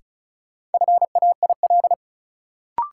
Morse code, heard as a keyed tone, about 36 wpm, 700 Hz. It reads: FAIL
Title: Sentences from Top 1100 Words - New Word Drill - 35wpm